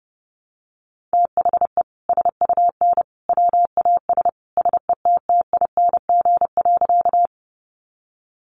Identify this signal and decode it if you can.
Morse code, heard as a keyed tone, about 30 wpm, 700 Hz. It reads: THE SUN WAS SETTING.
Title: Sentences from Top 1300 Words 30wpm